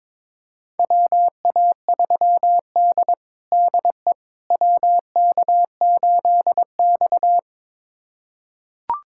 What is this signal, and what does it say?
Morse code, heard as a keyed tone, about 22 wpm, 700 Hz. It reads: WA3D DE WK8X
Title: USA Callsign DE Callsign - 4 Characters - 22wpm